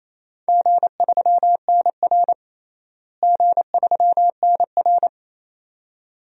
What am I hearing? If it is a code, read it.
Morse code, heard as a keyed tone, about 27 wpm, 700 Hz. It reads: G3NR G3NR
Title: DXCC-100 Call Signs - 4 Characters - Repeated 2x - 28wpm